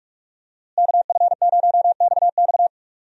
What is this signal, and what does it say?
Morse code, heard as a keyed tone, about 46 wpm, 700 Hz. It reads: KF0XX
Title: USA Call Signs - 5 Characters - Repeated 2x - 45wpm